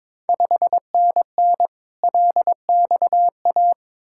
Morse code, heard as a keyed tone, about 22 wpm, 700 Hz. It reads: 5NN LXA
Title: Ontario QSO Party Exchange 22wpm